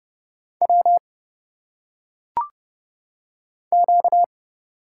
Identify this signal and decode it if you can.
Morse code, heard as a keyed tone, about 30 wpm, 700 Hz. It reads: W Q
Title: Single Letters - Phonetic Pronunciation 30wpm